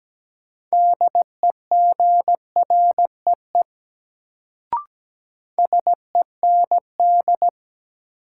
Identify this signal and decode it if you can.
Morse code, heard as a keyed tone, about 17 wpm, 700 Hz. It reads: DEGREE SEND
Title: Sentences from Top 1500 Words - New Word Drill Alt - 17wpm